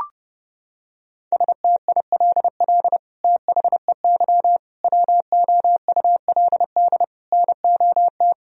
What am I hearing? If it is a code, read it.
Morse code, heard as a keyed tone, about 30 wpm, 700 Hz. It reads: STILL THEY WOULD NOT
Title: Sentences from Top 1000 Words - Review Part 1 - 30wpm